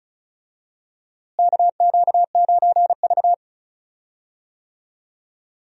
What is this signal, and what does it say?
Morse code, heard as a keyed tone, about 36 wpm, 700 Hz. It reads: KQ9V
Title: USA Call Signs - 4 Characters - Repeated 3x - 35wpm